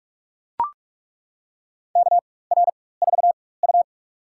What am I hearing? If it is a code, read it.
Morse code, heard as a keyed tone, about 41 wpm, spaced as for 20 wpm, 700 Hz. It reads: KR4U
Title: USA Call Signs - 4 Characters - 4x Character Spacing - 45wpm